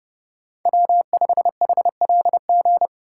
Morse code, heard as a keyed tone, about 34 wpm, 700 Hz. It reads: W5HLG
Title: Speed Racing - USA 1x3 Call Signs 30wpm